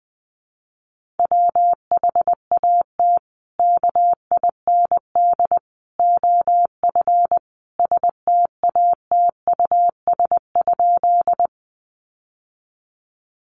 Morse code, heard as a keyed tone, about 20 wpm, 700 Hz. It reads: WHAT KIND OF STATUS?
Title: Sentences from Top 1400 Words 20wpm